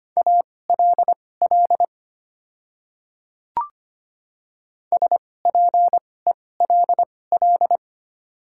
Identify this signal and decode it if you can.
Morse code, heard as a keyed tone, about 25 wpm, spaced as for 18 wpm, 700 Hz. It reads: ALL SPELL
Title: 2x Character Spacing - Top 200 Words - 25wpm